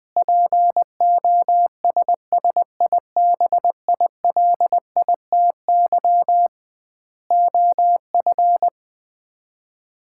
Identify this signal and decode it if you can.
Morse code, heard as a keyed tone, about 20 wpm, 700 Hz. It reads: POSSIBILITY OF
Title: 2x Word-Spacing - Sets of 3 Words - 20wpm